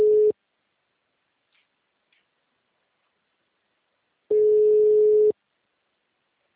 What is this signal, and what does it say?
Ringback tone (Europe, 425 Hz).